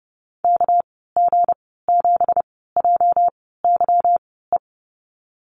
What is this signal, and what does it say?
Morse code, heard as a keyed tone, about 30 wpm, spaced as for 17 wpm, 700 Hz. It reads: KG7JYE
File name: Lesson 37g - Letter x - Call Signs 30wpm